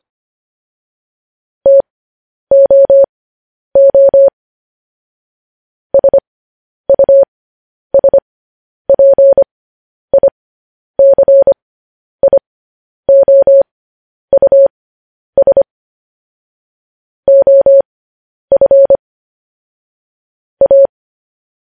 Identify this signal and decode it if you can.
Morse code, heard as a keyed tone, about 25 wpm, spaced as for 10 wpm, 550 Hz. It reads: TOO SUSPICIOUS OF A
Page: News Headlines in Morse Code at 10 WPM